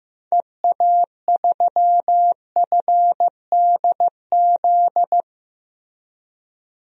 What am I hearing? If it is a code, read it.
Morse code, heard as a keyed tone, about 15 wpm, 700 Hz. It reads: EA3FDZ
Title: DXCC-100 Call Signs - 6 Characters - Repeated 3x - 15wpm